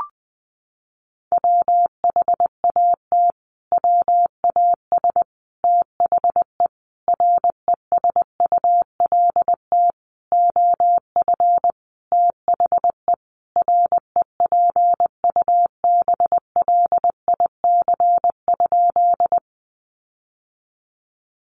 Morse code, heard as a keyed tone, about 19 wpm, 700 Hz. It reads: WHAT WAS THE RESULT OF THE REPUBLIC?
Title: Sentences from Top 1500 Words 20wpm